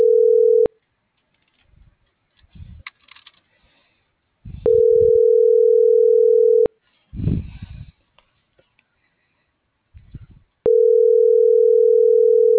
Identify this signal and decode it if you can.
Ringback tone (North America), 440 + 480 Hz.